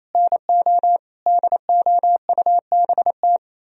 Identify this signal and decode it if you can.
Morse code, heard as a keyed tone, about 35 wpm, 700 Hz. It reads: NO DOUBT
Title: Speed-Racing - Sets of 5 Words 28wpm